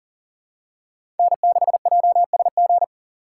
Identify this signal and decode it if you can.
Morse code, heard as a keyed tone, about 39 wpm, 700 Hz. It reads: N6JSG
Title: USA Call Signs - 5 Characters - 3x Character Spacing - 40wpm